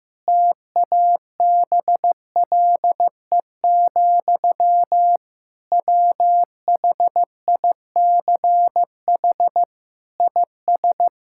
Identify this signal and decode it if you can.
Morse code, heard as a keyed tone, about 15 wpm, 700 Hz. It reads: TABLE, WHICH IS
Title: Sentences from Top 1600 Words 15wpm